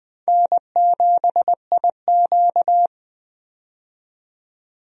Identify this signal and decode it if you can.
Morse code, heard as a keyed tone, about 20 wpm, 700 Hz. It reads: N7IQ
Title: USA Call Signs - 4 Characters - 20wpm